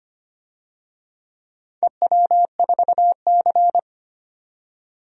Morse code, heard as a keyed tone, about 25 wpm, 700 Hz. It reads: EW4C
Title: DXCC-100 Call Signs - 4 Characters - 25wpm